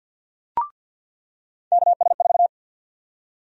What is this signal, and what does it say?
Morse code, heard as a keyed tone, about 49 wpm, 700 Hz. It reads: KS4
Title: USA Call Sign Prefixes - 50wpm